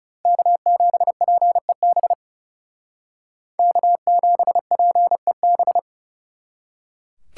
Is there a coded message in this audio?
Yes: Morse code from 0.3 to 5.8 s, heard as a keyed tone, about 40 wpm, 700 Hz. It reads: K7PEB K7PEB